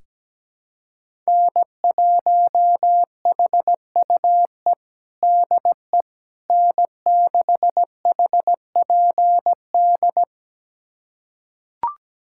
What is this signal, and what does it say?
Morse code, heard as a keyed tone, about 17 wpm, 700 Hz. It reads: N1HUE DE N6HPD
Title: USA Callsign DE Callsign 17wpm